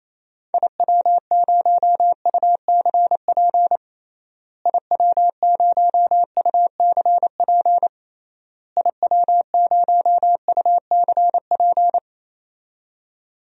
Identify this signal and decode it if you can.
Morse code, heard as a keyed tone, about 28 wpm, 700 Hz. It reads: IW0UCP IW0UCP IW0UCP
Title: DXCC-100 Call Signs - 6 Characters - Repeated 3x - 28wpm